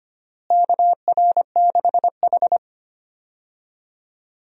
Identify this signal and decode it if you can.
Morse code, heard as a keyed tone, about 25 wpm, 700 Hz. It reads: KR6H
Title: USA Call Signs - 4 Characters - Repeated 3x - 25wpm